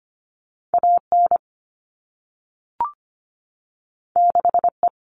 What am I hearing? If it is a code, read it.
Morse code, heard as a keyed tone, about 25 wpm, 700 Hz. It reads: AN 6E
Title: All Cut Numbers - 2-Digit 25wpm